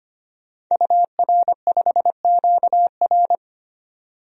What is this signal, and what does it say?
Morse code, heard as a keyed tone, about 25 wpm, 700 Hz. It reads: UR5QR
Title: DXCC-100 Call Signs - 5 Characters - 25wpm